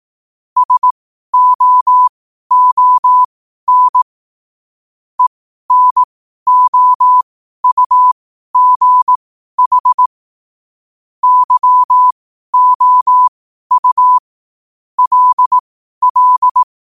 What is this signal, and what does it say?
Morse code, heard as a keyed tone, about 18 wpm, spaced as for 12 wpm, 1000 Hz. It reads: SOON ENOUGH YOU LL
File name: qotdfiles / 12wpm